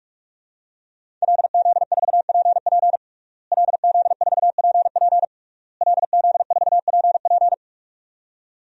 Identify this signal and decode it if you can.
Morse code, heard as a keyed tone, about 41 wpm, 700 Hz. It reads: LZ4PP LZ4PP LZ4PP
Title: DXCC-100 Call Signs - 5 Characters - Repeated 3x - 45wpm